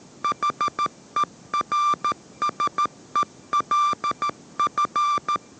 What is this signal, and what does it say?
Morse code, heard as a keyed tone, about 13 wpm, 1180 Hz. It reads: HERSELF